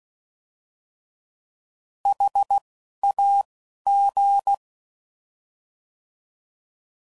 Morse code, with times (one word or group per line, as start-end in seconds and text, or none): 2.1-2.6 H
3.0-3.4 A
3.9-4.5 G